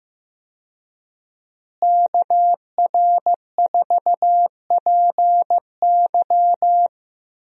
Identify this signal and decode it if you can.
Morse code, heard as a keyed tone, about 15 wpm, 700 Hz. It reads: KR4PY
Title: USA Call Signs - 5 Characters - Repeated 2x - 15wpm